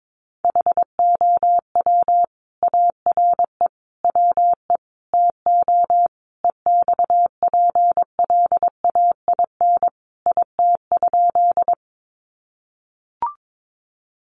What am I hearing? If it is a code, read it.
Morse code, heard as a keyed tone, about 22 wpm, 700 Hz. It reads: HOW ARE WE TO EXPLAIN IT?
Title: Sentences from Top 1100 Words 22wpm